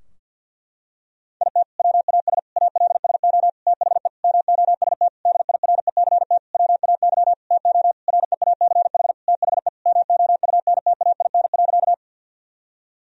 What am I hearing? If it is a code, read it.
Morse code, heard as a keyed tone, about 46 wpm, 700 Hz. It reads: IT WAS ALSO THE MOST DIRECT WAY TO REACH THE MOUNTAIN.
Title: Sentences from Top 1000 Words - Review Part 2 - 50wpm